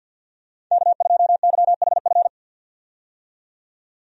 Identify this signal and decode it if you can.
Morse code, heard as a keyed tone, about 48 wpm, 700 Hz. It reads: K2YHF
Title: USA Call Signs - 5 Characters - 50wpm